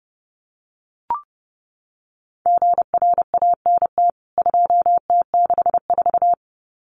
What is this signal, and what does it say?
Morse code, heard as a keyed tone, about 30 wpm, 700 Hz. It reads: GRANT 2T64
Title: CWT Exchange 30wpm